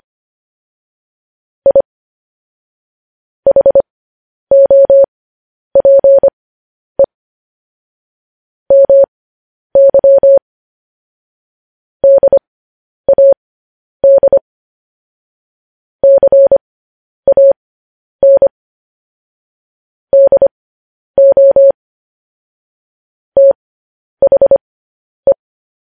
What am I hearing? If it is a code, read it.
Morse code, heard as a keyed tone, about 25 wpm, spaced as for 10 wpm, 550 Hz. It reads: I HOPE MY DAD CAN DO THE
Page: news headlines in morse code at 10 wpm